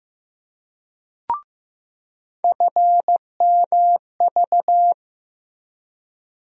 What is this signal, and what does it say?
Morse code, heard as a keyed tone, about 15 wpm, 700 Hz. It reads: FMV
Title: USA Call Sign Postfixes - 3 Letters 15wpm